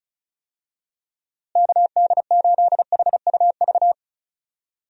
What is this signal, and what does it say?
Morse code, heard as a keyed tone, about 34 wpm, 700 Hz. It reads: KD8HUV
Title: USA Call Signs - 6 Characters - Repeated 2x - 35wpm